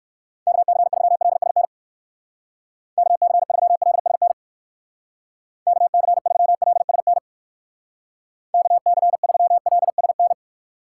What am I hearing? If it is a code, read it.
Morse code, heard as a keyed tone, about 56 wpm, 700 Hz. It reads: KC3LSN KC3LSN KC3LSN KC3LSN
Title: Speed Racing - USA 2x3 Call Signs 45wpm